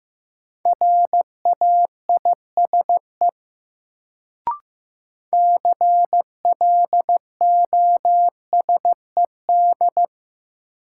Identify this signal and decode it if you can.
Morse code, heard as a keyed tone, about 15 wpm, 700 Hz. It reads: RAISE CLOSED
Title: Sentences from Top 1400 Words - New Word Drill - 15wpm